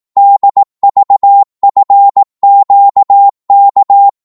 Morse code, heard as a keyed tone, about 18 wpm, 800 Hz. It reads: DVFQK
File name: Morse / ltr-18wpm